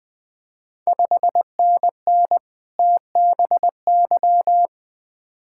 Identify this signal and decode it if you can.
Morse code, heard as a keyed tone, about 20 wpm, 700 Hz. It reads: 5NN TBY
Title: Ontario QSO Party Exchange 20wpm